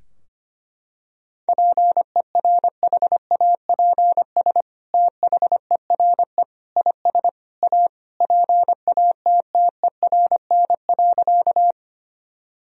Morse code, heard as a keyed tone, about 25 wpm, 700 Hz. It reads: PERHAPS THERE IS A PATTERN.
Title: Sentences from Top 1000 Words - Review Part 2 - 25wpm